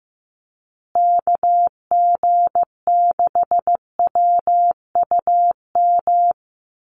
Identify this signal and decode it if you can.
Morse code, heard as a keyed tone, about 15 wpm, 700 Hz. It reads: KG6WUM
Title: USA Call Signs Repeated 3x 15wpm